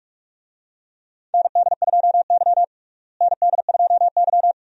Morse code, heard as a keyed tone, about 46 wpm, 700 Hz. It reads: ND2Y ND2Y
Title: USA Call Signs - 4 Characters - Repeated 2x - 45wpm